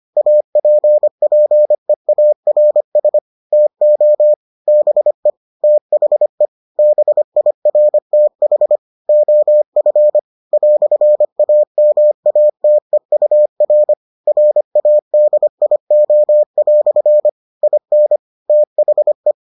Morse code, heard as a keyed tone, about 25 wpm, 600 Hz. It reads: APPEARS TO BE THE BIRTH OF "AMATEUR RADIO" IN THE